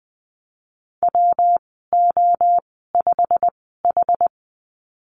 Morse code, heard as a keyed tone, about 20 wpm, spaced as for 14 wpm, 700 Hz. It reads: WO5H
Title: 122 Lesson 20g - Number 3 - Call Signs 20wpm